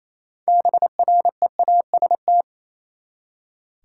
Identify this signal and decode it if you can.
Morse code, heard as a keyed tone, about 28 wpm, 700 Hz. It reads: BREAST
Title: Six Letter Words - 28wpm